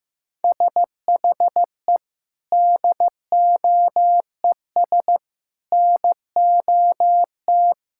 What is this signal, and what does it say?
Morse code, heard as a keyed tone, about 15 wpm, 700 Hz. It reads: SHE DOES NOT
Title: Sentences from Top 1600 Words 15wpm